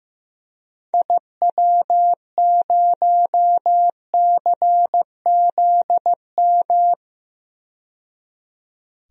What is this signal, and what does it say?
Morse code, heard as a keyed tone, about 15 wpm, 700 Hz. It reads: IW0CZM
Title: DXCC-100 Call Signs - 6 Characters - 15wpm